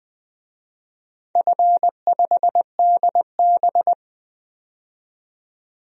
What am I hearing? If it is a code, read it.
Morse code, heard as a keyed tone, about 20 wpm, 700 Hz. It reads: F5DB
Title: DXCC-100 Call Signs - 4 Characters - 20wpm